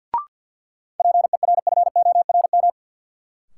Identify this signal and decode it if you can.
Morse code, heard as a keyed tone, about 53 wpm, 700 Hz. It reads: PERFORM
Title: Sentences from Top 1500 Words - New Word Drill Alt - 50wpm